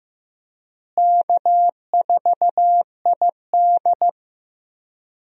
Morse code, heard as a keyed tone, about 16 wpm, 700 Hz. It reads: K4ID
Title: USA Call Signs - 4 Characters - Repeated 2x - 15wpm